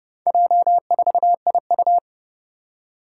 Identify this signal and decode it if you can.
Morse code, heard as a keyed tone, about 28 wpm, 690 Hz. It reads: J4IU
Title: Four Challenging Letter-Number Combinations 30wpm